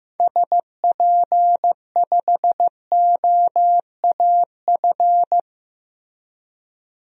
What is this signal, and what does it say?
Morse code, heard as a keyed tone, about 15 wpm, 700 Hz. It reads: SP5OAF
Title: DXCC-100 Call Signs - 6 Characters - 15wpm